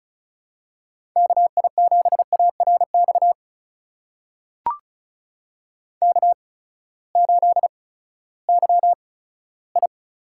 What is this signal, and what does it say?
Morse code, heard as a keyed tone, about 36 wpm, spaced as for 18 wpm, 700 Hz. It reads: KI7ARX K8YI
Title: USA Call Signs - 8x Character Spacing - 35wpm